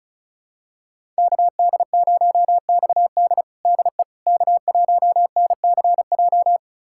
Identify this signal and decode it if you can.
Morse code, heard as a keyed tone, about 36 wpm, 700 Hz. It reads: KD0XD DE K1NCJ
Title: USA Callsign DE Callsign - 5 Characters - 35wpm